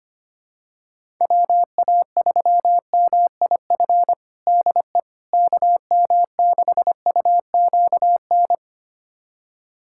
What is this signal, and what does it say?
Morse code, heard as a keyed tone, about 25 wpm, 700 Hz. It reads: WA3MIF DE KM6UQN
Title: USA Callsign DE Callsign - 6 Characters - 25wpm